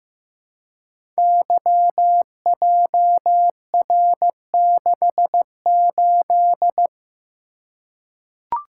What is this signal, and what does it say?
Morse code, heard as a keyed tone, about 15 wpm, 700 Hz. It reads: YJR68
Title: Code Groups - 5 Letter:Numbers - 15wpm